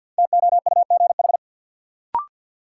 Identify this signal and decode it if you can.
Morse code, heard as a keyed tone, about 53 wpm, 700 Hz. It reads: TOUGH
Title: Sentences from Top 1200 Words - New Word Drill - 50wpm